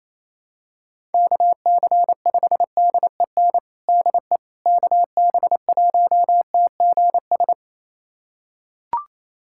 Morse code, heard as a keyed tone, about 28 wpm, 700 Hz. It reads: KC5DEN DE KB1TGS
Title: USA Callsign DE Callsign - 6 Characters - 28wpm